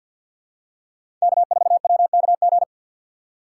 Morse code, heard as a keyed tone, about 47 wpm, 700 Hz. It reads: K4WKG
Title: USA Call Signs - 2x Character Spacing - 50wpm